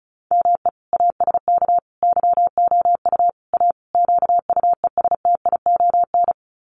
Morse code, heard as a keyed tone, about 34 wpm, 700 Hz. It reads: ME ASK YOU A QUESTION
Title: Sets of 5 Words Encore2 35wpm